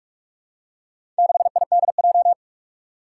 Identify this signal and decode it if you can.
Morse code, heard as a keyed tone, about 44 wpm, 700 Hz. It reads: 6IDJ
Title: Four Challenging Letter-Number Combinations 45wpm